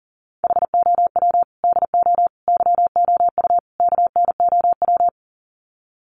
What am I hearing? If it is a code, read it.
Morse code, heard as a keyed tone, about 39 wpm, 700 Hz. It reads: HOW DO YOU KNOW
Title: Sets of 4 Words 40wpm